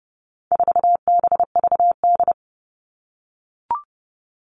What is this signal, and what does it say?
Morse code, heard as a keyed tone, about 30 wpm, 700 Hz. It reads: 4BVD
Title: All Cut Numbers - 4-Digit 30wpm